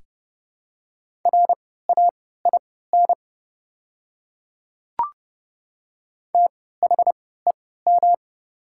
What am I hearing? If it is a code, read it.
Morse code, heard as a keyed tone, about 30 wpm, spaced as for 17 wpm, 700 Hz. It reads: RAIN THEM
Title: 3x Character Spacing - Top 500 Words - 30wpm